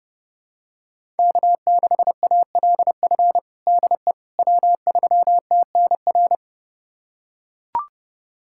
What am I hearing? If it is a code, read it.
Morse code, heard as a keyed tone, about 32 wpm, 700 Hz. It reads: K6ALF DE W3TNR
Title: USA Callsign DE Callsign - 5 Characters - 30wpm